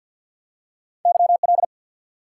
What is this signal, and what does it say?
Morse code, heard as a keyed tone, about 53 wpm, 700 Hz. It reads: YL
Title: Top 90 QSO Elements 50wpm